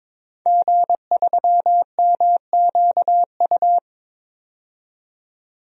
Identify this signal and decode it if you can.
Morse code, heard as a keyed tone, about 22 wpm, 700 Hz. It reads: G3MQU
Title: DXCC-100 Call Signs - 5 Characters - 22wpm